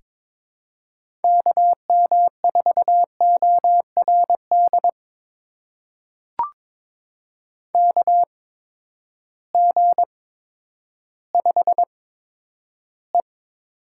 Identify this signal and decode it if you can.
Morse code, heard as a keyed tone, about 22 wpm, spaced as for 11 wpm, 700 Hz. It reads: KM4ORD KG5E
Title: USA Call Signs - 8x Character Spacing - 22wpm